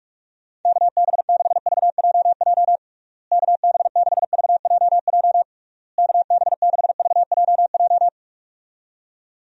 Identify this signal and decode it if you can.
Morse code, heard as a keyed tone, about 43 wpm, 700 Hz. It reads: KB6VJJ KB6VJJ KB6VJJ
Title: USA Call Signs - 6 Characters - Repeated 3x - 45wpm